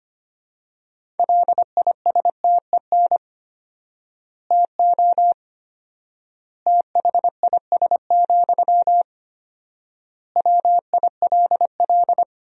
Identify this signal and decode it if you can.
Morse code, heard as a keyed tone, about 25 wpm, 700 Hz. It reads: LISTEN TO THIS, WILL
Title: 4x Word-Spacing - Sentences from Top 400 Words - 25wpm